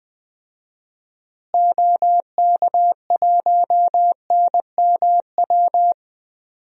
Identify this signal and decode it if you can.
Morse code, heard as a keyed tone, about 20 wpm, 700 Hz. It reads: OK1NMW
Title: DXCC-100 Call Signs - 6 Characters - Repeated 3x - 20wpm